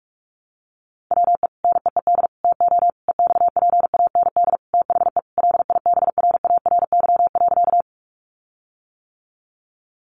Morse code, heard as a keyed tone, about 44 wpm, 700 Hz. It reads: WE NEED TO EXPAND THE LIBRARY.
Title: Sentences from Top 1400 Words 45wpm